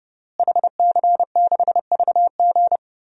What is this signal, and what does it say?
Morse code, heard as a keyed tone, about 30 wpm, 700 Hz. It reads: HC6VG
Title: DXCC-100 Call Signs - 5 Characters - Repeated 3x - 30wpm